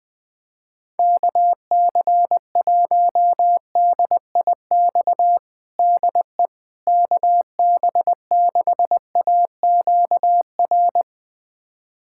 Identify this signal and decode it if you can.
Morse code, heard as a keyed tone, about 20 wpm, 700 Hz. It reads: KC1DIX DE KB6AQR
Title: USA Callsign DE Callsign 20wpm